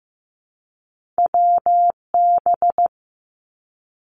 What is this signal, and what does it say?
Morse code, heard as a keyed tone, about 15 wpm, 700 Hz. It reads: WB
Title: Two Challenging Letter-Number Combinations 15wpm